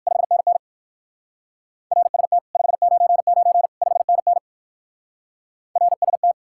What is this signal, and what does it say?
Morse code, heard as a keyed tone, about 57 wpm, 700 Hz. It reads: 5NN RST 599 5NN RST